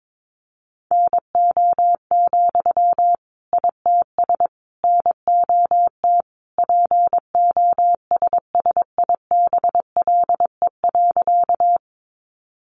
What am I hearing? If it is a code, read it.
Morse code, heard as a keyed tone, about 22 wpm, 700 Hz. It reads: NO, ITS NOT POSSIBLE.
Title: Sentences from Top 600 Words 22wpm